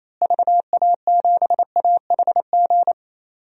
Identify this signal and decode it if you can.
Morse code, heard as a keyed tone, about 28 wpm, 700 Hz. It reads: VA7AHG
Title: DXCC-100 Call Signs - 6 Characters - Repeated 2x - 28wpm